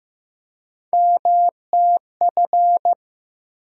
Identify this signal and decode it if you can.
Morse code, heard as a keyed tone, about 15 wpm, 700 Hz. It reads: MTF
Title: USA Call Sign Postfixes - 3 Letters 15wpm